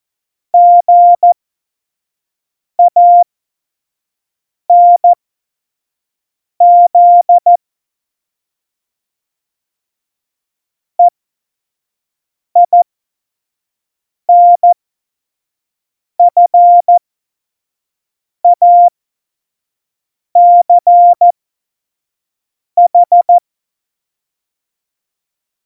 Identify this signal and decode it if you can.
Morse code, heard as a keyed tone, about 14 wpm, spaced as for 5 wpm, 700 Hz. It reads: GANZ EINFACH
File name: Morsekurs / Witze / Witz 11. Februar 2020, 25 F70